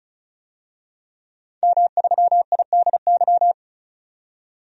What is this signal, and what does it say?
Morse code, heard as a keyed tone, about 34 wpm, 700 Hz. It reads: M3IDY
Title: DXCC-100 Call Signs - 5 Characters - 35wpm